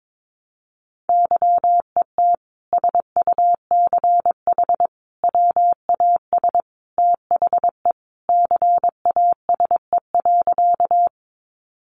Morse code, heard as a keyed tone, about 22 wpm, 700 Hz. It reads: YET SUCH WAS THE CASE.
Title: Sentences from Top 700 Words 22wpm